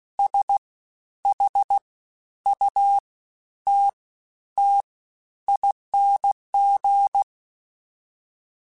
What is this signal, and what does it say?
Morse code, heard as a keyed tone, about 16 wpm, 800 Hz. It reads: S H U T T ING
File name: Special INGs-mp3s / Special INGs-mp3s / Fast ing @15